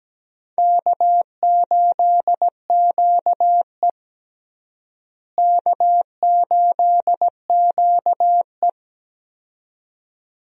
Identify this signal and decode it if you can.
Morse code, heard as a keyed tone, about 17 wpm, 700 Hz. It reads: K8QE K8QE
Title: USA Call Signs - 4 Characters - Repeated 2x - 17wpm